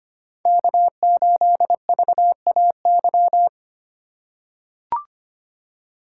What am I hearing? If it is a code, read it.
Morse code, heard as a keyed tone, about 25 wpm, 700 Hz. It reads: K8VAY
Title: USA Call Signs - 5 Characters - 2x Character Spacing - 25wpm